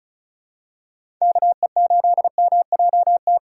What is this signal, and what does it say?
Morse code, heard as a keyed tone, about 36 wpm, 700 Hz. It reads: KE8MJT
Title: USA Call Signs - 2x Character Spacing - 35wpm